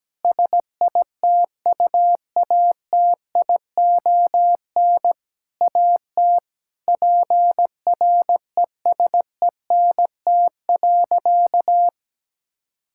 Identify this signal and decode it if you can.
Morse code, heard as a keyed tone, about 17 wpm, 700 Hz. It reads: SITUATION AT PRESENT.
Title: Sentences from Top 1600 Words 17wpm